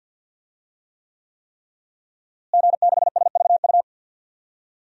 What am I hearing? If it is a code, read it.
Morse code, heard as a keyed tone, about 52 wpm, 700 Hz. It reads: G6SVU
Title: DXCC-100 Call Signs - 5 Characters - Repeated 3x - 50wpm